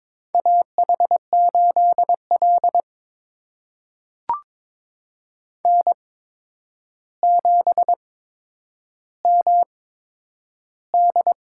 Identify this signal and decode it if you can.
Morse code, heard as a keyed tone, about 22 wpm, spaced as for 9 wpm, 700 Hz. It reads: AH8L N7MD
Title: USA Call Signs - 4 Characters - 8x Character Spacing - 22wpm